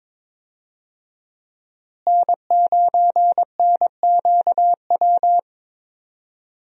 Morse code, heard as a keyed tone, about 22 wpm, 700 Hz. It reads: N9NQW
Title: USA Call Signs - 5 Characters - Repeated 3x - 22wpm